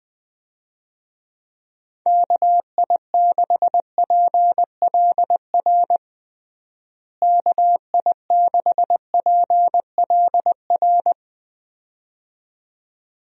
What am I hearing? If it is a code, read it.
Morse code, heard as a keyed tone, about 19 wpm, 700 Hz. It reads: KI6PLR KI6PLR
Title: USA Call Signs - 6 Characters - Repeated 2x - 20wpm